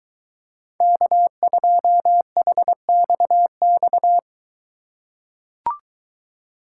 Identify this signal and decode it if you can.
Morse code, heard as a keyed tone, about 21 wpm, 700 Hz. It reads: K2HXX